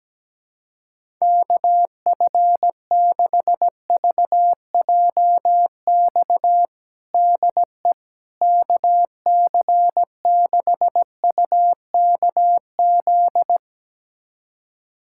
Morse code, heard as a keyed tone, about 17 wpm, 700 Hz. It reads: KF6VJX DE KC6UKZ
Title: USA Callsign DE Callsign - 6 Characters - 17wpm